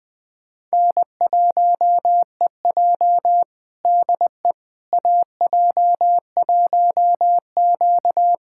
Morse code, heard as a keyed tone, about 21 wpm, 700 Hz. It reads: N1EJ DE AJ1Q
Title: USA Callsign DE Callsign - 4 Characters - 20wpm